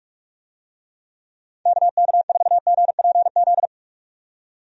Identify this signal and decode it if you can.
Morse code, heard as a keyed tone, about 41 wpm, 700 Hz. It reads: KK4GPZ
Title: USA Call Signs - 6 Characters - 45wpm